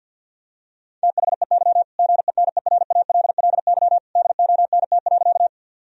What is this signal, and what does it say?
Morse code, heard as a keyed tone, about 50 wpm, 700 Hz. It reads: THEY GENERALLY DONT.
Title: Sentences from Top 1000 Words - Review Part 4 - 50wpm